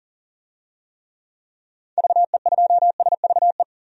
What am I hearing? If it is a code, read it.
Morse code, heard as a keyed tone, about 40 wpm, 700 Hz. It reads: VE2SVE
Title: DXCC-100 Call Signs - 6 Characters - 40wpm